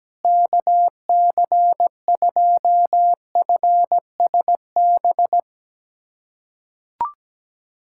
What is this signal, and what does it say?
Morse code, heard as a keyed tone, about 17 wpm, 700 Hz. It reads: KC2FSB
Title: USA Callsign DE Callsign 17wpm